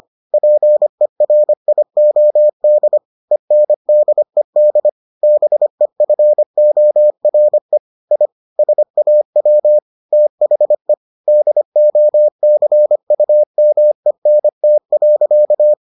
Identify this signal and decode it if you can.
Morse code, heard as a keyed tone, about 25 wpm, 600 Hz. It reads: PERIOD ENDED BEFORE I SAW THE DOCUMENT.